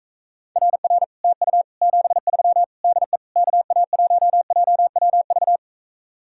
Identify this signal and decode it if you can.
Morse code, heard as a keyed tone, about 44 wpm, 700 Hz. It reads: RR TU 73 DE KA1JWV